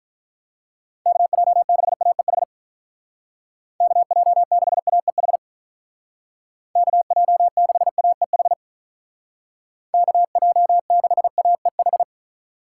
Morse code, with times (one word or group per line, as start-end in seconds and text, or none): 1.1-2.4 KJ6AEH
3.8-5.4 KJ6AEH
6.8-8.5 KJ6AEH
9.9-12.0 KJ6AEH